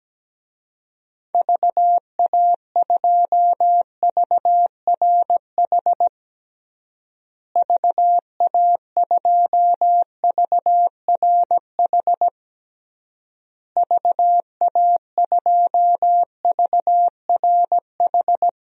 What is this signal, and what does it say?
Morse code, heard as a keyed tone, about 17 wpm, 700 Hz. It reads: VA2VRH VA2VRH VA2VRH
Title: DXCC-100 Call Signs - 6 Characters - Repeated 3x - 17wpm